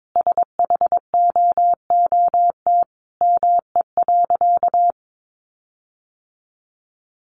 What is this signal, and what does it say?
Morse code, heard as a keyed tone, about 22 wpm, 700 Hz. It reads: SHOOT ME.
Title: Sentences from Top 1500 Words 22wpm